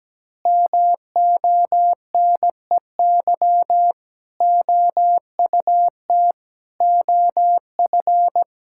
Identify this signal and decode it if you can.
Morse code, heard as a keyed tone, about 17 wpm, 700 Hz. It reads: MONEY OUT OF
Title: Sentences from Top 1000 Words - Review Part 5 - 17wpm